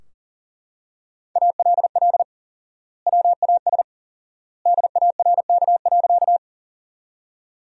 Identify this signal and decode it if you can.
Morse code, heard as a keyed tone, about 40 wpm, 700 Hz. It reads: ALL WAS DARK.
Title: 4x Word-Spacing - Sentences from Top 500 Words - 40wpm